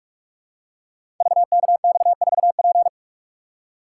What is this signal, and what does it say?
Morse code, heard as a keyed tone, about 44 wpm, 700 Hz. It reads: VKX4P
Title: Code Groups - 5 Letter:Numbers - 45wpm